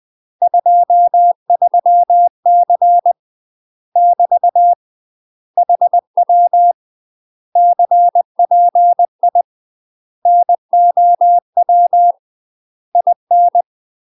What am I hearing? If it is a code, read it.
Morse code, heard as a keyed tone, about 20 wpm, 700 Hz. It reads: 23C <BT> HW CPI NOW IN
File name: UK MASTER QSOs TRAINING/20WPM